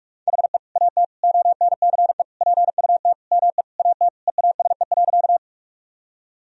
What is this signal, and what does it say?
Morse code, heard as a keyed tone, about 41 wpm, 700 Hz. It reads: HE AT ONCE PUT ME AT EASE.
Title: Sentences from Top 1000 Words - Review Part 1 - 45wpm